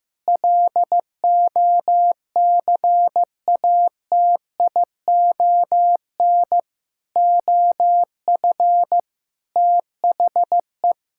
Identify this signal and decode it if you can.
Morse code, heard as a keyed tone, about 15 wpm, 700 Hz. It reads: LOCATION OF THE
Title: Sentences from Top 1000 Words - Review Part 6 - 15wpm